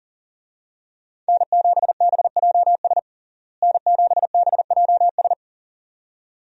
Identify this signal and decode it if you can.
Morse code, heard as a keyed tone, about 40 wpm, 700 Hz. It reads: N7BJS N7BJS
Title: USA Call Signs - 5 Characters - Repeated 2x - 40wpm